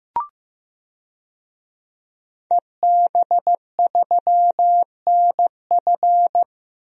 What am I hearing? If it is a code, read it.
Morse code, heard as a keyed tone, about 16 wpm, 700 Hz. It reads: EB3NF
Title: DXCC-100 Call Signs - 5 Characters - Repeated 3x - 15wpm